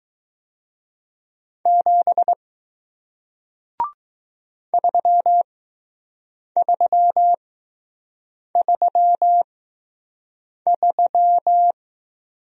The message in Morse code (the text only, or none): 7 3333